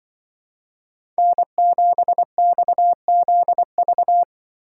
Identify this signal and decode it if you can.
Morse code, heard as a keyed tone, about 24 wpm, 700 Hz. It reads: N7XZV
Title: Speed Racing - USA 1x3 Call Signs 20wpm